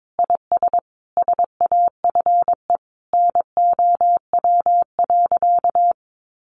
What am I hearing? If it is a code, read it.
Morse code, heard as a keyed tone, about 22 wpm, 700 Hz. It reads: IS SAFE NOW.
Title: Sentences from Top 900 Words 22wpm